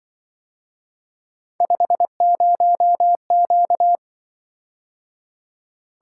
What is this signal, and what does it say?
Morse code, heard as a keyed tone, about 25 wpm, 700 Hz. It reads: 50Q
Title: Speed-Racing - Three Letter-Number Combinations - 20wpm